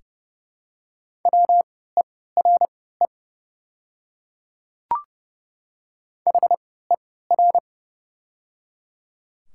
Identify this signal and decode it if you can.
Morse code, heard as a keyed tone, about 30 wpm, spaced as for 17 wpm, 700 Hz. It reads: WERE HER
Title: 3x Character Spacing - Top 100 Words - 30wpm